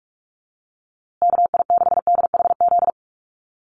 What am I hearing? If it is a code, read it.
Morse code, heard as a keyed tone, about 44 wpm, 700 Hz. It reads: KI6DHZ
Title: USA Call Signs Repeated 2x 45wpm